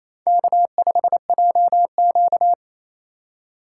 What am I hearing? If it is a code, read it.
Morse code, heard as a keyed tone, about 28 wpm, 700 Hz. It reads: K5JQ
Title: USA Call Signs - 4x Character Spacing - 28wpm